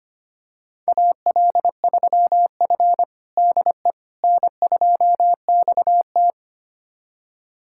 Morse code, heard as a keyed tone, about 25 wpm, 700 Hz. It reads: AL3F DE N2XT
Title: USA Callsign DE Callsign - 4 Characters - 25wpm